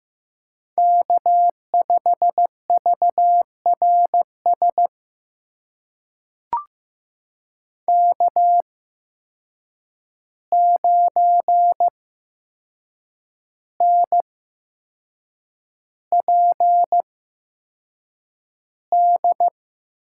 Morse code, heard as a keyed tone, about 16 wpm, spaced as for 6 wpm, 700 Hz. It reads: K5VRS K9NPD
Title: USA Call Signs - 5 Characters - 8x Character Spacing - 15wpm